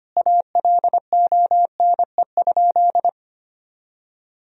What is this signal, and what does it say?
Morse code, heard as a keyed tone, about 25 wpm, 700 Hz. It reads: ALONE?
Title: Sentences from Top 1000 Words - Review Part 2 - 25wpm